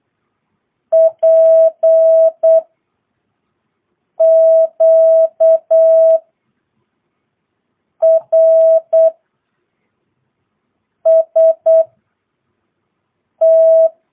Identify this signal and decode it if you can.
Morse code, heard as a keyed tone, about 8 wpm, spaced as for 4 wpm, 650 Hz. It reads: PQRST
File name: jh8cbh / 2022cw